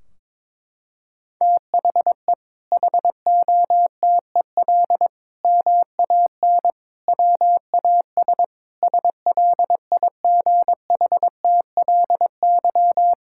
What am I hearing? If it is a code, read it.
Morse code, heard as a keyed tone, about 22 wpm, 700 Hz. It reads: THE HOTEL MAN WAS SLIGHTLY
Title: Sentences from Top 1600 Words 22wpm